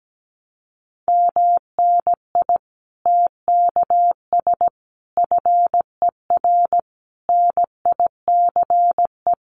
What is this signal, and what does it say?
Morse code, heard as a keyed tone, about 17 wpm, 700 Hz. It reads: MNI TKS FER NICE
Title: Common QSO Phrases 17wpm